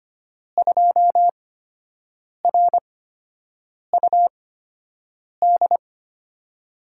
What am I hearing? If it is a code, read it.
Morse code, heard as a keyed tone, about 25 wpm, spaced as for 7 wpm, 700 Hz. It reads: 2RUD
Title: USA Call Signs - 5 Characters - 8x Character Spacing - 25wpm